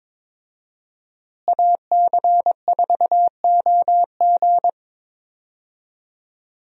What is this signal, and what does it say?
Morse code, heard as a keyed tone, about 22 wpm, 700 Hz. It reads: AC4OG
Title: USA Call Signs - 5 Characters - Repeated 3x - 22wpm